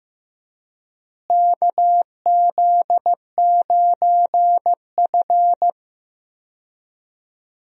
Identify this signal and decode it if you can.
Morse code, heard as a keyed tone, about 15 wpm, 700 Hz. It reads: KZ9F
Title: USA Call Signs - 4 Characters - Repeated 2x - 15wpm